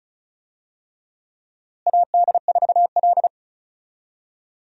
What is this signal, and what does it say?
Morse code, heard as a keyed tone, about 36 wpm, 700 Hz. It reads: AD4L
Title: USA Call Signs - 4 Characters - Repeated 3x - 35wpm